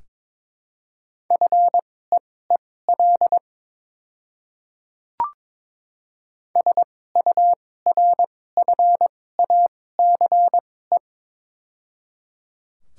Morse code, heard as a keyed tone, about 22 wpm, spaced as for 16 wpm, 690 Hz. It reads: FEEL SURFACE
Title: 2x Character Spacing - Top 500 Words - 22wpm